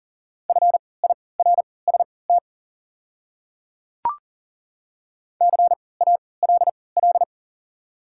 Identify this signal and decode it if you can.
Morse code, heard as a keyed tone, about 40 wpm, spaced as for 23 wpm, 690 Hz. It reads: FIRST CALL
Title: 3x Character Spacing - Top 100 Words - 40wpm